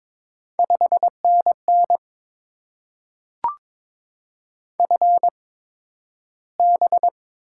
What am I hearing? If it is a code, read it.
Morse code, heard as a keyed tone, about 22 wpm, spaced as for 12 wpm, 700 Hz. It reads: 5NN FB
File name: Top 20 QSO Elements - 8x Character Spacing - 22wpm